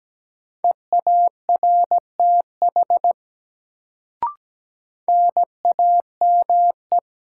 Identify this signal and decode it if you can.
Morse code, heard as a keyed tone, about 17 wpm, 700 Hz. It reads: EARTH NAME
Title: Sentences from Top 200 Words - New Word Drill - 17wpm